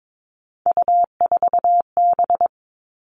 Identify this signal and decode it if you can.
Morse code, heard as a keyed tone, about 22 wpm, 700 Hz. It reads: U4B
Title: Three Challenging Letter-Number Combinations 22wpm